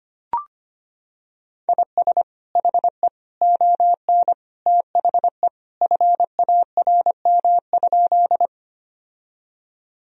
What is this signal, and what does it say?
Morse code, heard as a keyed tone, about 25 wpm, 700 Hz. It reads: IS HE ON THE FARM?
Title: Sentences from Top 1000 Words - Review Part 2 - 25wpm